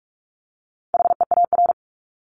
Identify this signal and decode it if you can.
Morse code, heard as a keyed tone, about 44 wpm, 700 Hz. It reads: HEAR
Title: Top 300 Words Repeated 45wpm